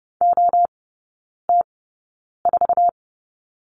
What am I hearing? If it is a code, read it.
Morse code, heard as a keyed tone, about 30 wpm, spaced as for 9 wpm, 700 Hz. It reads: OT4
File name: 081 Lesson 14e - Number 2 - Three Characters 30wpm